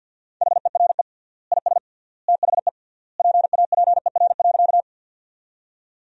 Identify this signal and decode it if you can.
Morse code, heard as a keyed tone, about 52 wpm, 700 Hz. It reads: HERE IS THE PAPER.
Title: 3x Word-Spacing - Sentences from Top 300 Words - 50wpm